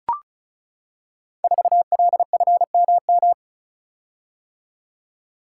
Morse code, heard as a keyed tone, about 36 wpm, 700 Hz. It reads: 4LFMM
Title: Code Groups - 5 Letter:Numbers - 35wpm